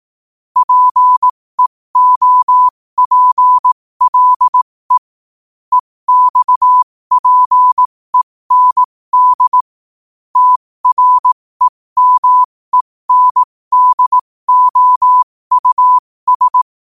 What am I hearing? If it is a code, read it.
Morse code, heard as a keyed tone, about 18 wpm, spaced as for 15 wpm, 1000 Hz. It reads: PEOPLE EXPEND TREMENDOUS
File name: qotdfiles / 15wpm